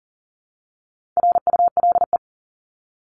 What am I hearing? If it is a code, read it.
Morse code, heard as a keyed tone, about 40 wpm, 700 Hz. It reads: RULE
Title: ES:S Suffix - Top 100 Words with Base Words 40wpm